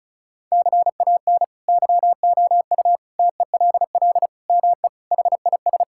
Morse code, heard as a keyed tone, about 34 wpm, 700 Hz. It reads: CAN YOU TELL ME HIS
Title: Sentences from Top 1600 Words 35wpm